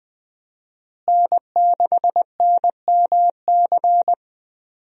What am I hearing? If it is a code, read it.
Morse code, heard as a keyed tone, about 21 wpm, 700 Hz. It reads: N6NMC
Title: USA Call Signs - 5 Characters - Repeated 2x - 20wpm